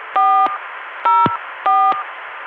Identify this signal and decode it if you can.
DTMF, heard as telephone touch tones, tones of about 380 ms. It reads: 404